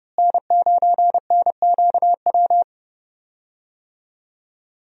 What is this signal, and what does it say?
Morse code, heard as a keyed tone, about 30 wpm, 700 Hz. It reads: N9NQW
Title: USA Call Signs - 5 Characters - Repeated 3x - 30wpm